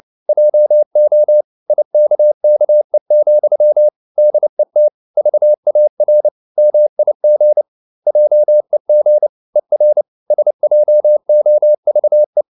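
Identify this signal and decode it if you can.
Morse code, heard as a keyed tone, about 29 wpm, 600 Hz. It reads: JO IKKE, DET VAR MIG JEG ER SJOVE
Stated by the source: Fast (QRQ) Morse at 29 wpm, from CW med Gnister.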